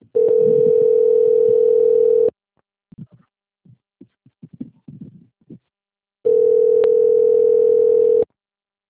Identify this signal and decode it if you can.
Ringback tone (North America), 440 + 480 Hz.